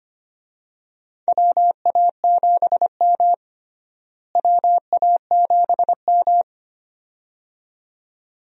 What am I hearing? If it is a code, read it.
Morse code, heard as a keyed tone, about 25 wpm, 700 Hz. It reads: WA7M WA7M
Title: USA Call Signs - 4 Characters - Repeated 2x - 25wpm